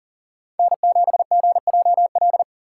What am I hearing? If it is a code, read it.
Morse code, heard as a keyed tone, about 37 wpm, 700 Hz. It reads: N7GJL
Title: USA Call Signs - 3x Character Spacing - 40wpm